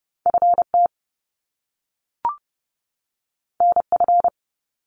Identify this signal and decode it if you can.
Morse code, heard as a keyed tone, about 30 wpm, 700 Hz. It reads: FT NF
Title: Top Two Letter Combinations in Words 30wpm